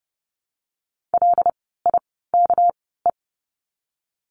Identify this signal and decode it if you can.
Morse code, heard as a keyed tone, about 30 wpm, spaced as for 17 wpm, 700 Hz. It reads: LIKE
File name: course / 30:10wpm / 230 Lesson 37f - Letter x - Words 30wpm